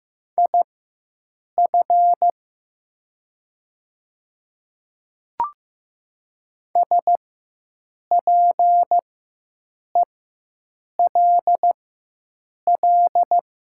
Morse code, heard as a keyed tone, about 15 wpm, spaced as for 7 wpm, 700 Hz. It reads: IF SPELL
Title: 4x Character Spacing - Top 200 Words - 15wpm